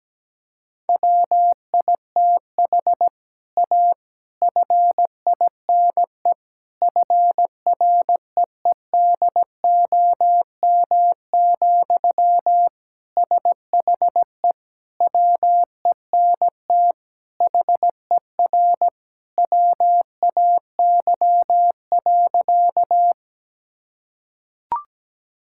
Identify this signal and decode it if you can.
Morse code, heard as a keyed tone, about 17 wpm, 700 Hz. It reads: WITH A FINE FREEDOM, SHE WENT HER WAY.
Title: Sentences from Top 1000 Words - Review Part 5 - 17wpm